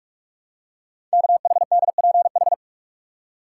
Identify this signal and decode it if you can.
Morse code, heard as a keyed tone, about 44 wpm, 700 Hz. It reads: KHDPH
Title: Code Groups - 5 Letters - 45wpm